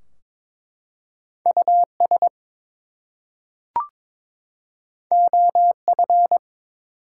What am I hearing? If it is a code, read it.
Morse code, heard as a keyed tone, about 22 wpm, 700 Hz. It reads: US OF
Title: Two Letter Words - 22wpm